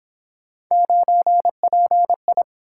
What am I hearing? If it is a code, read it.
Morse code, heard as a keyed tone, about 26 wpm, 700 Hz. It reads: 9PI
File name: Speed-Racing - Three Letter-Number Combinations - 22wpm